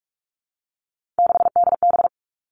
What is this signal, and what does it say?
Morse code, heard as a keyed tone, about 46 wpm, 700 Hz. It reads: 6DB
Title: Three Challenging Letter-Number Combinations 45wpm